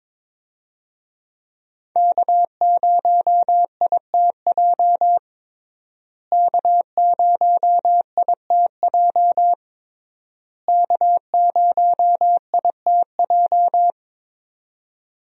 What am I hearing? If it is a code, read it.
Morse code, heard as a keyed tone, about 21 wpm, 700 Hz. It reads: K0ITJ K0ITJ K0ITJ
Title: USA Call Signs - 5 Characters - Repeated 3x - 22wpm